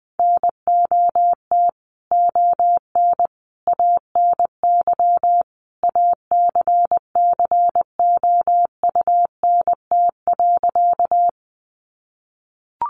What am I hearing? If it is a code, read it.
Morse code, heard as a keyed tone, about 20 wpm, 700 Hz. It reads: NOT ON ANY ACCOUNT.
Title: Sentences from Top 900 Words 20wpm